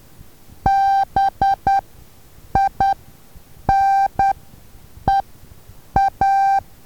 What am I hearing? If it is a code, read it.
Morse code, heard as a keyed tone, about 10 wpm, spaced as for 7 wpm, 780 Hz. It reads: BINEA
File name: Taviro HA6VA B